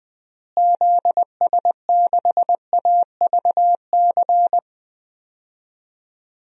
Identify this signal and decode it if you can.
Morse code, heard as a keyed tone, about 21 wpm, 700 Hz. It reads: ZS6AVC
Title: DXCC-100 Call Signs - 6 Characters - Repeated 3x - 20wpm